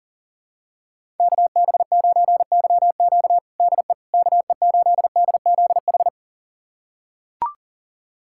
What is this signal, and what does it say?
Morse code, heard as a keyed tone, about 37 wpm, 700 Hz. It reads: KB9YQ DE KE8DZH
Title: USA Callsign DE Callsign 40wpm